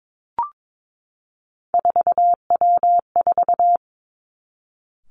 Morse code, heard as a keyed tone, about 22 wpm, 700 Hz. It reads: 4W4
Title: Three Challenging Letter-Number Combinations 22wpm